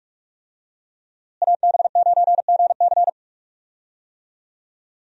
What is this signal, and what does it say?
Morse code, heard as a keyed tone, about 46 wpm, 700 Hz. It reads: AB9GC
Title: USA Call Signs - 5 Characters - Repeated 2x - 45wpm